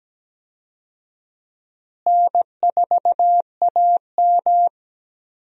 Morse code, heard as a keyed tone, about 17 wpm, 700 Hz. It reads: N4AM
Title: USA Call Signs - 4 Characters - Repeated 3x - 17wpm